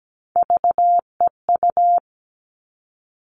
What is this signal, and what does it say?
Morse code, heard as a keyed tone, about 17 wpm, 700 Hz. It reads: VEU
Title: All Cut Numbers - 3-Digit 17wpm